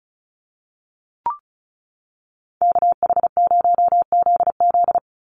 Morse code, heard as a keyed tone, about 34 wpm, 700 Hz. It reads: KH0ZZ
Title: Call Signs - 35wpm